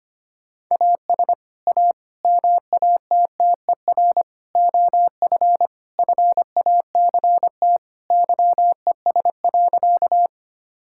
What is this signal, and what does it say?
Morse code, heard as a keyed tone, about 25 wpm, 700 Hz. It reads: AS A MATTER OF FACT YES.
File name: practice / 25wpm / Sentences with 4-Word Phrases - 25wpm